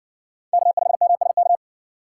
Morse code, heard as a keyed tone, about 60 wpm, 700 Hz. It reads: K5MSK